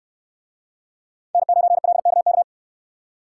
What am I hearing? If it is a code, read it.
Morse code, heard as a keyed tone, about 69 wpm, 700 Hz. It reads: N9LKK